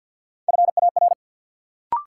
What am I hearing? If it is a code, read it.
Morse code, heard as a keyed tone, about 47 wpm, 700 Hz. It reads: FAR
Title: Sentences from Top 300 Words - New Word Drill - 50wpm